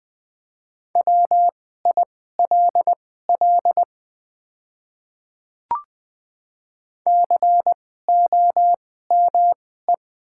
Morse code, heard as a keyed tone, about 20 wpm, spaced as for 14 wpm, 700 Hz. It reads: WILL COME
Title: 2x Character Spacing - Top 100 Words - 20wpm